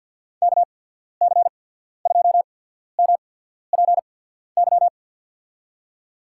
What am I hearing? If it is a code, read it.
Morse code, heard as a keyed tone, about 54 wpm, 700 Hz. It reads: K C 2 M P Y
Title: USA Call Signs - 6 Characters - 8x Character Spacing - 50wpm